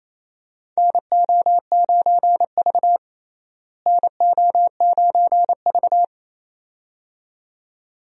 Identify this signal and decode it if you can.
Morse code, heard as a keyed tone, about 28 wpm, 700 Hz. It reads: NO9V NO9V
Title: USA Call Signs - 4 Characters - Repeated 2x - 28wpm